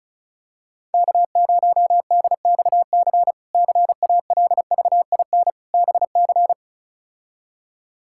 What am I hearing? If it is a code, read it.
Morse code, heard as a keyed tone, about 34 wpm, 700 Hz. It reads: K0DXC CALVIN BC
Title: K1USN SST QSO Practice - 35wpm